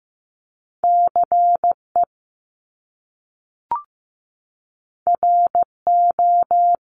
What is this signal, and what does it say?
Morse code, heard as a keyed tone, about 15 wpm, 700 Hz. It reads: CE RO
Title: Top Two Letter Combinations in Words 15wpm